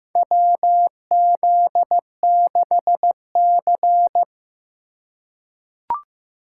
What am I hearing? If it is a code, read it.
Morse code, heard as a keyed tone, about 15 wpm, 700 Hz. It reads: WZ6C
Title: USA Call Signs - 4 Characters - 6x Character Spacing - 15wpm